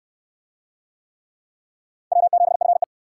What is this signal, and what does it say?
Morse code, heard as a keyed tone, about 68 wpm, 700 Hz. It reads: W7FE